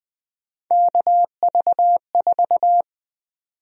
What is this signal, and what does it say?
Morse code, heard as a keyed tone, about 20 wpm, 700 Hz. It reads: KV4
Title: USA Call Sign Prefixes - 20wpm